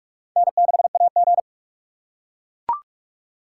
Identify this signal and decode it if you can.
Morse code, heard as a keyed tone, about 44 wpm, 700 Hz. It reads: N6AG
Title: USA Call Signs - 4 Characters - 2x Character Spacing - 45wpm